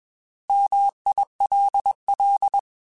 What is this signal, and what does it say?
Morse code, heard as a keyed tone, about 21 wpm, 800 Hz. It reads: MILL
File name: Alphabetical / Alphabetical mp3s / Ms @20